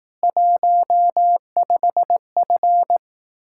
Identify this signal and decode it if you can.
Morse code, heard as a keyed tone, about 18 wpm, 700 Hz. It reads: 15F
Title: Speed-Racing Common QSO Phrases Encore - 15wpm